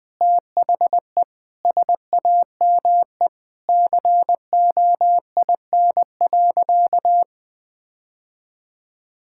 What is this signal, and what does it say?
Morse code, heard as a keyed tone, about 20 wpm, 700 Hz. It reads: THE SAME COIN.
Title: Sentences with 7-Word Phrases - 20wpm